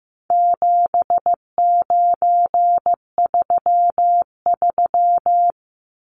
Morse code, heard as a keyed tone, about 15 wpm, 700 Hz. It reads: 7933
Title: Numbers - 4-Digit 15wpm